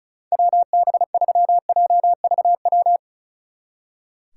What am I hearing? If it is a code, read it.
Morse code, heard as a keyed tone, about 45 wpm, 700 Hz. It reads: WB3JVW